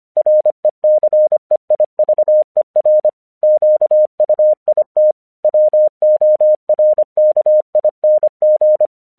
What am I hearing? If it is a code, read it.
Morse code, heard as a keyed tone, about 25 wpm, 610 Hz. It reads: RECEIVER QUIT WORKING
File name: 1 English 25wpm / AQC13